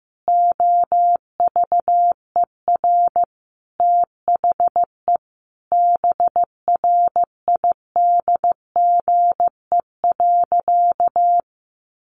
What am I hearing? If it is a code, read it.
Morse code, heard as a keyed tone, about 15 wpm, 700 Hz. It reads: OVER THE BRIDGE.
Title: Sentences from Top 1400 Words 15wpm